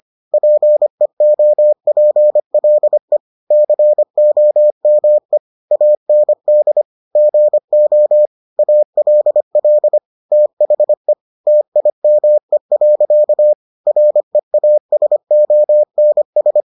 Morse code, heard as a keyed tone, about 25 wpm, 600 Hz. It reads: PEOPLE COME AND GO ALL THE TIME. REASONS